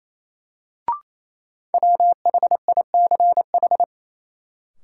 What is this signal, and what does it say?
Morse code, heard as a keyed tone, about 28 wpm, 700 Hz. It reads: WHICH
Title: Sentences from Top 100 Words - New Word Drill - 28wpm